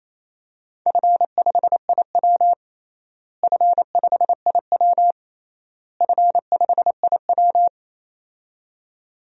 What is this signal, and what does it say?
Morse code, heard as a keyed tone, about 27 wpm, 700 Hz. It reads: F5IW F5IW F5IW
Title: DXCC-100 Call Signs - 4 Characters - Repeated 3x - 28wpm